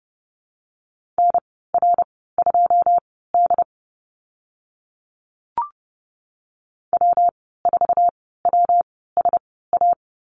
Morse code, heard as a keyed tone, about 30 wpm, spaced as for 17 wpm, 700 Hz. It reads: NR2D W4WSA
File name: Lesson 19g - Letter w - Call Signs 30wpm